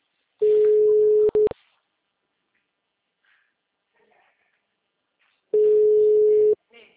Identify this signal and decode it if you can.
Ringback tone (Europe, 425 Hz), with speech.